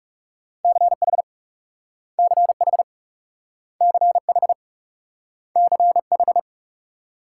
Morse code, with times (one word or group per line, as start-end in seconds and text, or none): 0.6-1.2 CH
2.2-2.8 CH
3.8-4.5 CH
5.6-6.4 CH